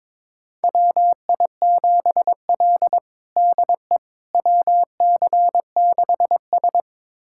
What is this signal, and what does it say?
Morse code, heard as a keyed tone, about 22 wpm, 700 Hz. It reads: WI7L DE WC6S
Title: USA Callsign DE Callsign - 4 Characters - 22wpm